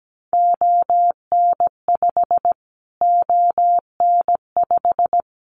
Morse code, heard as a keyed tone, about 17 wpm, 700 Hz. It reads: ON5 ON5
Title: Callsign Prefixes for Top 25 DXCC - Repeated 3x - 17wpm